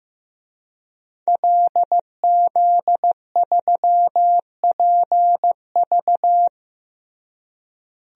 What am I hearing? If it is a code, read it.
Morse code, heard as a keyed tone, about 15 wpm, 700 Hz. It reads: LZ3PV
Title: DXCC-100 Call Signs - 5 Characters - Repeated 2x - 15wpm